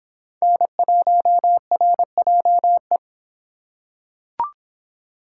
Morse code, heard as a keyed tone, about 25 wpm, 700 Hz. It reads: N1RJE